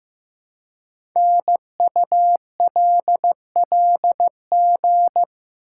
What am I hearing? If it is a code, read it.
Morse code, heard as a keyed tone, about 15 wpm, 700 Hz. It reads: NULLG
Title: Code Groups - 5 Letter:Numbers - 15wpm